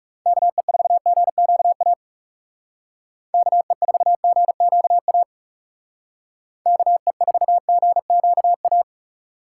Morse code, heard as a keyed tone, about 36 wpm, 700 Hz. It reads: KE4GQA KE4GQA KE4GQA
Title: Speed Racing - USA 2x3 Call Signs 30wpm